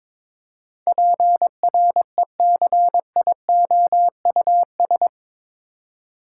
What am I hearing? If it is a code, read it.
Morse code, heard as a keyed tone, about 22 wpm, 700 Hz. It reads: PRECIOUS
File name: practice / 22wpm / Eight Letter Words - 22wpm